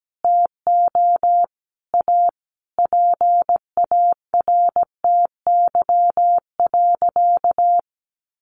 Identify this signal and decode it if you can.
Morse code, heard as a keyed tone, about 17 wpm, 700 Hz. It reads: TO A PARTY.
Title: Easy Phrases 17wpm